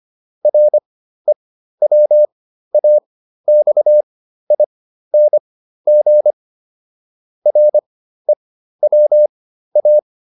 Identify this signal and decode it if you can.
Morse code, heard as a keyed tone, about 25 wpm, spaced as for 13 wpm, 600 Hz. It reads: REWAXING REWA